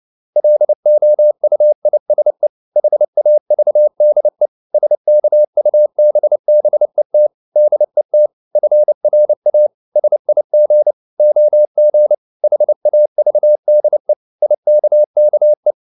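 Morse code, heard as a keyed tone, about 29 wpm, 600 Hz. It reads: LOUISE HAVDE SKUBBET DET FRA SIG OG HAVDE IKKE
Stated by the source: Fast (QRQ) Morse at 29 wpm, from CW med Gnister.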